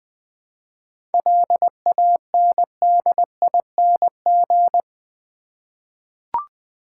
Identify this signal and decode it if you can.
Morse code, heard as a keyed tone, about 20 wpm, 690 Hz. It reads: LANDING
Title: 7 Letter Words - First 3 Letters Given - 20wpm